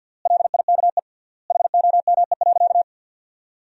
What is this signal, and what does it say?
Morse code, heard as a keyed tone, about 49 wpm, 700 Hz. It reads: LIKE HOME.
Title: 3x Word-Spacing - Sentences from Top 200 Words - 50wpm